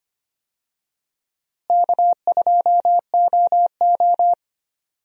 Morse code, heard as a keyed tone, about 25 wpm, 700 Hz. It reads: K2OO
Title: USA Call Signs - 4 Characters - Repeated 2x - 25wpm